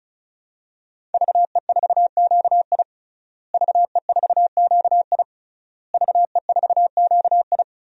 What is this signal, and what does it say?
Morse code, heard as a keyed tone, about 36 wpm, 700 Hz. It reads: VE4QI VE4QI VE4QI
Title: DXCC-100 Call Signs - 5 Characters - Repeated 3x - 35wpm